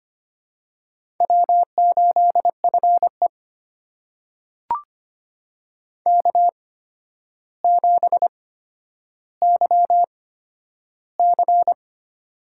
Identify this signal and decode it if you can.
Morse code, heard as a keyed tone, about 25 wpm, spaced as for 11 wpm, 700 Hz. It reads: W8FE K7YC
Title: USA Call Signs - 4 Characters - 8x Character Spacing - 25wpm